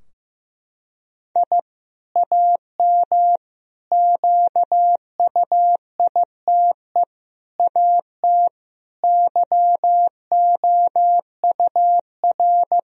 Morse code, heard as a keyed tone, about 15 wpm, 700 Hz. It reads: I AM QUITE AT YOUR
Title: Sentences from Top 1000 Words - Review Part 4 - 15wpm